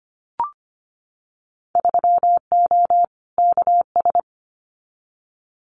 Morse code, heard as a keyed tone, about 25 wpm, 700 Hz. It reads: 3O KS
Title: Winter Field Day Exchanges 25wpm